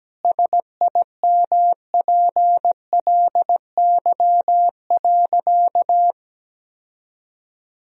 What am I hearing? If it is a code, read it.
Morse code, heard as a keyed tone, about 17 wpm, 700 Hz. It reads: SIMPLY.
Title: Sentences from Top 1000 Words - Review Part 4 - 17wpm